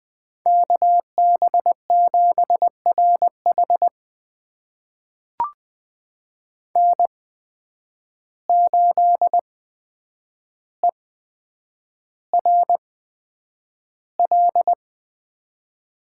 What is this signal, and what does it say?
Morse code, heard as a keyed tone, about 20 wpm, spaced as for 9 wpm, 700 Hz. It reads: KB7RH N8ERL
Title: USA Call Signs - 8x Character Spacing - 20wpm